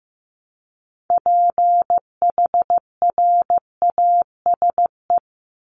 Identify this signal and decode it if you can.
Morse code, heard as a keyed tone, about 15 wpm, 700 Hz. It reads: PHRASE
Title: Top 1501-1750 Words 15wpm